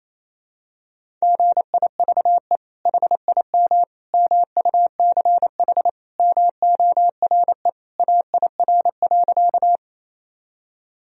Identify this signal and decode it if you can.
Morse code, heard as a keyed tone, about 28 wpm, 700 Hz. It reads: GIVE HIM MUCH MORE AIR.
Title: Sentences from Top 1000 Words - Review Part 1 - 28wpm